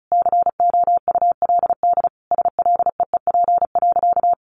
Morse code, heard as a keyed tone, about 34 wpm, 700 Hz. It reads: COULD SLEEP.
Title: Sentences from Top 500 Words 35wpm